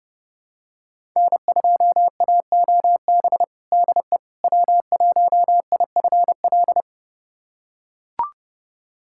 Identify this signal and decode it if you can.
Morse code, heard as a keyed tone, about 30 wpm, 700 Hz. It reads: N2AOB DE W1IFL
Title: USA Callsign DE Callsign - 5 Characters - 30wpm